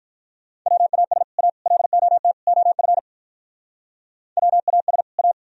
Morse code, heard as a keyed tone, about 45 wpm, 700 Hz. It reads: WAS A LOT OF WAS A